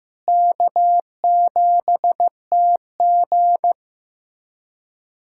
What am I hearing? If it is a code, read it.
Morse code, heard as a keyed tone, about 15 wpm, 700 Hz. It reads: K7TG
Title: USA Callsign DE Callsign - 4 Characters - 15wpm